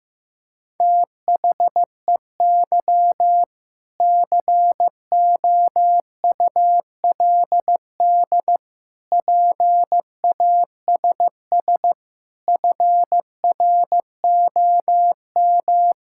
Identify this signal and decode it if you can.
Morse code, heard as a keyed tone, about 16 wpm, 700 Hz. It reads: THEY COULD PASS FROM
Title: Sentences from Top 1000 Words - Review Part 6 - 15wpm